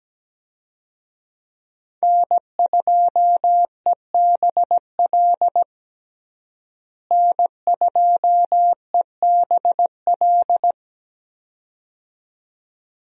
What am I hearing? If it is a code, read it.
Morse code, heard as a keyed tone, about 17 wpm, 700 Hz. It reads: N2EBL N2EBL
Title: USA Call Signs - 5 Characters - Repeated 2x - 17wpm